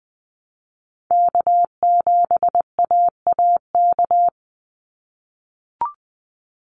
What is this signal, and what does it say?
Morse code, heard as a keyed tone, about 20 wpm, 700 Hz. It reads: K7AAK
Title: USA Call Signs 20wpm